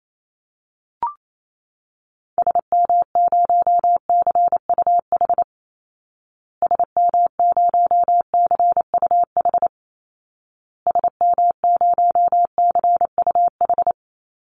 Morse code, heard as a keyed tone, about 28 wpm, 700 Hz. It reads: SM0CUH SM0CUH SM0CUH
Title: Call Signs Repeated 2x - 28wpm